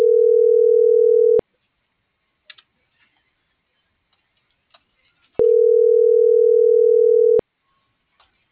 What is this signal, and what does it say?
Ringback tone (North America), 440 + 480 Hz.